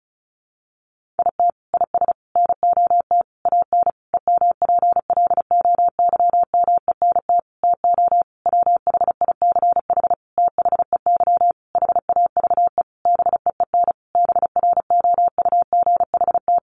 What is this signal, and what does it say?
Morse code, heard as a keyed tone, about 34 wpm, 700 Hz. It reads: IT IS NOT AN EMPLOYMENT TO WHICH THEY HAVE BEEN BROUGHT
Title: Sentences from Top 1500 Words 35wpm